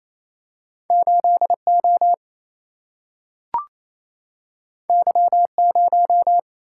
Morse code, heard as a keyed tone, about 28 wpm, 700 Hz. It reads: 8 O Y 0
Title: Single-Letter Single-Number 28wpm